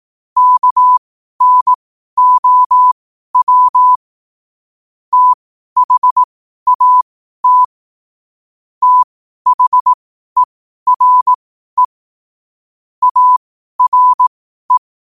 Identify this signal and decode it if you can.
Morse code, heard as a keyed tone, about 18 wpm, spaced as for 12 wpm, 1000 Hz. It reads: KNOW THAT THERE ARE
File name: qotdfiles / 12wpm